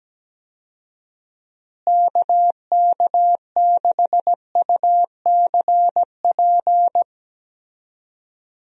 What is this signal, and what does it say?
Morse code, heard as a keyed tone, about 17 wpm, 700 Hz. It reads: KK6UCP
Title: USA Call Signs - 6 Characters - Repeated 2x - 17wpm